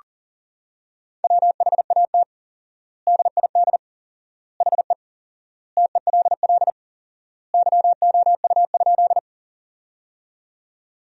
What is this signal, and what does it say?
Morse code, heard as a keyed tone, about 40 wpm, 700 Hz. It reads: WHAT DID HE TELL YOU?
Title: 4x Word-Spacing - Sentences from Top 200 Words - 40wpm